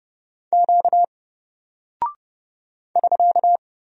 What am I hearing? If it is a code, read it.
Morse code, heard as a keyed tone, about 29 wpm, 700 Hz. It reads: Q <SK>
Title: Single Character (Letter:Number:Punctuation:Prosign) 30wpm